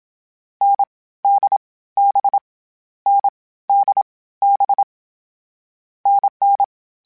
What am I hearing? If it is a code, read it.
Morse code, heard as a keyed tone, about 27 wpm, 800 Hz. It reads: N D B N D B NN
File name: Short Runs / Short Runs-mp3s / NDB NDB6 @25